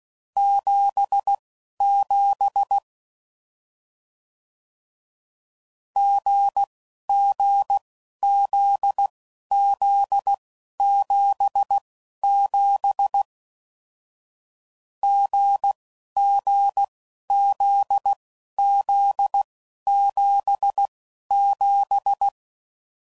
Morse code, heard as a keyed tone, about 16 wpm, spaced as for 11 wpm, 800 Hz. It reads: 77 GGZZ77 GGZZ77
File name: Combinations / Difficult Combos / GZ GZ7